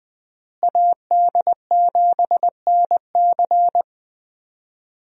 Morse code, heard as a keyed tone, about 21 wpm, 700 Hz. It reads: AD7NC
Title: USA Call Signs - 5 Characters - Repeated 3x - 20wpm